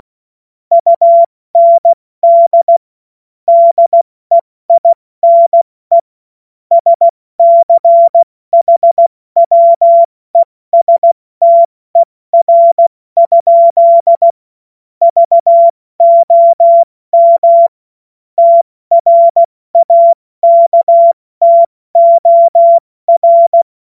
Morse code, heard as a keyed tone, about 16 wpm, 680 Hz. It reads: UND DEINE SCHWESTER? VOM TRAKTOR
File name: Morsekurs / Witze / Witz 12. Oktober 2021, 70 F80